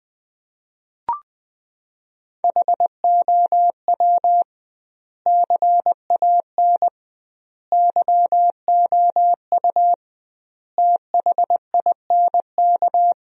Morse code, heard as a keyed tone, about 20 wpm, 700 Hz. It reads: HOW CAN YOU THINK
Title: 2x Word-Spacing - Sentences from Top 200 Words - 20wpm